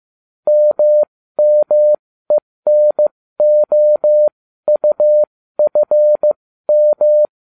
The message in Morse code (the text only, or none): MMENOUFM